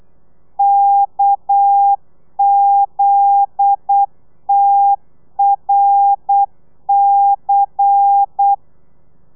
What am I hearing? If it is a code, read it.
Morse code, heard as a keyed tone, about 8 wpm, 800 Hz. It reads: KZTRC
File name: telegrafia / i2tfj / Lez10 35